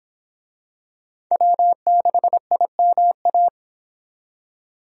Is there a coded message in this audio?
Yes: Morse code from 1.3 to 3.5 s, heard as a keyed tone, about 26 wpm, 700 Hz. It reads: W6IMA